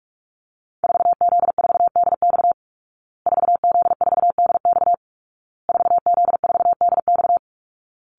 Morse code, heard as a keyed tone, about 44 wpm, 700 Hz. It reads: 4Z4DX 4Z4DX 4Z4DX
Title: Call Signs Repeated 2x - 45wpm